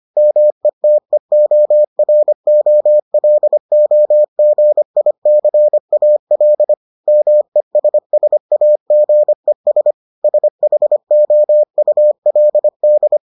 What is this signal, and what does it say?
Morse code, heard as a keyed tone, about 25 wpm, 600 Hz. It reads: METEOROLOGICAL MESSAGES SHOULD